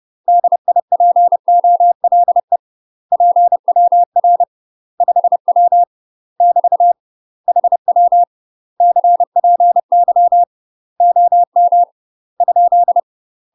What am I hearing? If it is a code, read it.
Morse code, heard as a keyed tone, about 30 wpm, 700 Hz. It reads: DIPOLE PWR 5W <BT> HW CPY OM ?
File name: UK MASTER QSOs TRAINING / 30WPM / UK MASTER QSOs 0031